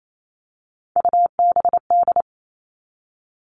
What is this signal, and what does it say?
Morse code, heard as a keyed tone, about 28 wpm, 700 Hz. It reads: UBD
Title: All Cut Numbers - 3-Digit 28wpm